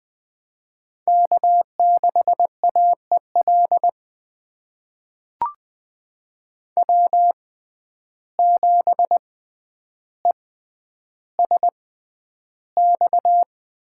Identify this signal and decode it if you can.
Morse code, heard as a keyed tone, about 20 wpm, spaced as for 10 wpm, 700 Hz. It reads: K6AEL W7ESX
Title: USA Call Signs - 5 Characters - 6x Character Spacing - 20wpm